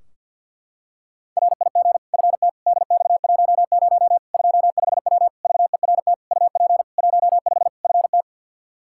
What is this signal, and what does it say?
Morse code, heard as a keyed tone, about 49 wpm, 700 Hz. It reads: RIG FT DX10 25W VERT UP 15 FT
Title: K1USN SST QSO Practice 50wpm